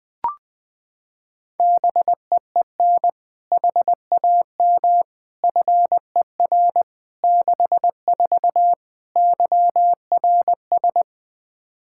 Morse code, heard as a keyed tone, about 20 wpm, 700 Hz. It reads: BEEN HAM FER 64 YRS
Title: K1USN SST QSO Practice 20wpm